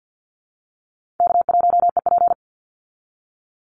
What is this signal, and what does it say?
Morse code, heard as a keyed tone, about 50 wpm, 700 Hz. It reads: K1EP
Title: Easy Call Signs - 50wpm